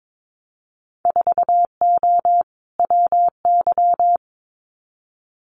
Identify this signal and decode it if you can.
Morse code, heard as a keyed tone, about 22 wpm, 700 Hz. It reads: 4O WY
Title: Winter Field Day Exchanges Encore 22wpm